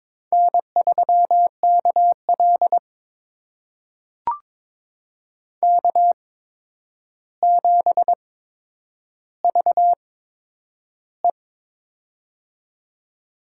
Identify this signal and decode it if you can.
Morse code, heard as a keyed tone, about 23 wpm, spaced as for 10 wpm, 700 Hz. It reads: N3KL K7VE
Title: USA Call Signs - 4 Characters - 8x Character Spacing - 22wpm